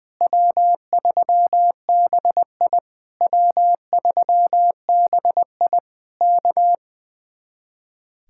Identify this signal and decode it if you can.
Morse code, heard as a keyed tone, about 20 wpm, 700 Hz. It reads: W3BI W3BI K
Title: Speed-Racing Common QSO Phrases Encore - 20wpm